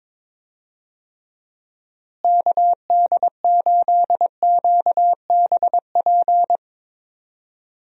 Morse code, heard as a keyed tone, about 22 wpm, 700 Hz. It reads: KD8QBP
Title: USA Call Signs - 6 Characters - Repeated 3x - 22wpm